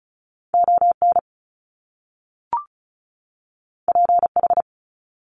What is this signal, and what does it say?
Morse code, heard as a keyed tone, about 36 wpm, 700 Hz. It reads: ON PH
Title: Top Two Letter Combinations in Words 35wpm